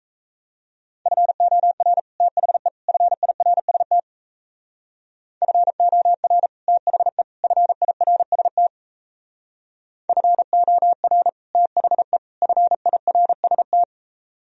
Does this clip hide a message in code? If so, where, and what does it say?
Yes: Morse code from 1.1 to 13.8 s, heard as a keyed tone, about 36 wpm, 700 Hz. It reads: FOR THE FIRST FOR THE FIRST FOR THE FIRST